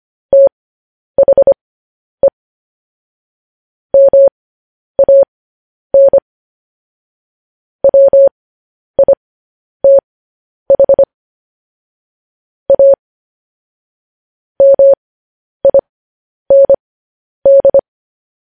Morse code, heard as a keyed tone, about 25 wpm, spaced as for 10 wpm, 550 Hz. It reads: THE MAN WITH A MIND
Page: news-headlines-in-morse-code-at-10-wpm